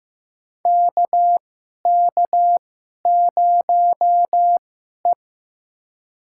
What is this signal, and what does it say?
Morse code, heard as a keyed tone, about 15 wpm, spaced as for 11 wpm, 700 Hz. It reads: KK0E
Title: USA Call Signs - 4 Characters - 2x Character Spacing - 15wpm